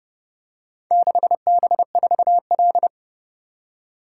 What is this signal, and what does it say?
Morse code, heard as a keyed tone, about 29 wpm, 700 Hz. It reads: 6B4L